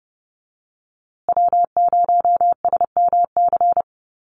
Morse code, heard as a keyed tone, about 30 wpm, 700 Hz. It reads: W0SMC
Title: USA Call Signs Repeated 2x 30wpm